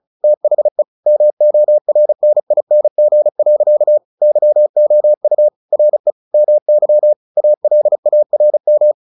Morse code, heard as a keyed tone, about 34 wpm, 600 Hz. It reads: THE MORNING. YOU RE MY ALARM
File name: the old man and the sea / the old man and the sea 1 35wpm